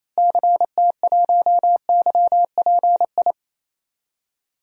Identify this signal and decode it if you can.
Morse code, heard as a keyed tone, about 28 wpm, 700 Hz. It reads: CT1YPI
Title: DXCC-100 Call Signs - 6 Characters - Repeated 3x - 28wpm